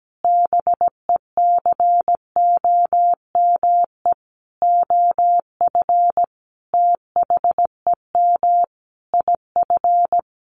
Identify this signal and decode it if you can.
Morse code, heard as a keyed tone, about 17 wpm, 700 Hz. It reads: BECOME OF THEM IF
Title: Sentences from Top 1100 Words 17wpm